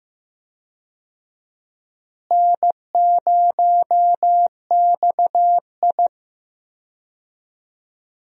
Morse code, heard as a keyed tone, about 15 wpm, 700 Hz. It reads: N0XI
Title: USA Call Signs - 4 Characters - Repeated 3x - 15wpm